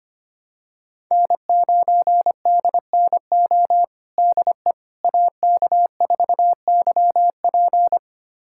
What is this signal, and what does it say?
Morse code, heard as a keyed tone, about 25 wpm, 700 Hz. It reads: N9DNO DE AK4YP
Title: USA Callsign DE Callsign - 5 Characters - 25wpm